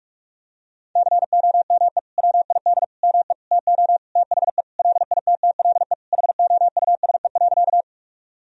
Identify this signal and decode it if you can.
Morse code, heard as a keyed tone, about 46 wpm, 700 Hz. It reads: COME WID ME TO THE LITTLE HOUSE.
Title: Sentences from Top 200 Words and Abbreviations 45wpm